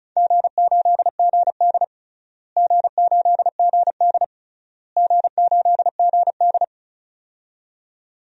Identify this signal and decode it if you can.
Morse code, heard as a keyed tone, about 34 wpm, 700 Hz. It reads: G8GD G8GD G8GD
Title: DXCC-100 Call Signs - 4 Characters - Repeated 3x - 35wpm